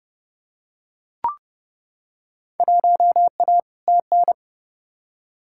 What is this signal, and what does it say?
Morse code, heard as a keyed tone, about 31 wpm, 700 Hz. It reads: 1A TN
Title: Field Day Exchanges - 30wpm